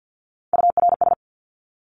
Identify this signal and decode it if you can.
Morse code, heard as a keyed tone, about 49 wpm, 700 Hz. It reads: URS
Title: Abbreviations 50wpm